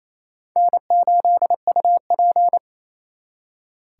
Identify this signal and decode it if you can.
Morse code, heard as a keyed tone, about 36 wpm, 700 Hz. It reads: N8UP